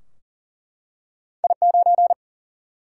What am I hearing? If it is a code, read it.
Morse code, heard as a keyed tone, about 40 wpm, 700 Hz. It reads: I 9
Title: Single-Letter Single-Number 40wpm